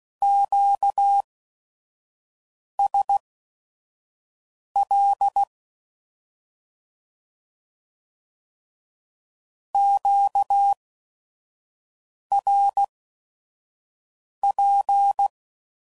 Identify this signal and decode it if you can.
Morse code, heard as a keyed tone, about 16 wpm, spaced as for 5 wpm, 800 Hz. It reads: QSL QRP
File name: CW Code Groups @5WPM / Q Calls / Q Calls Twice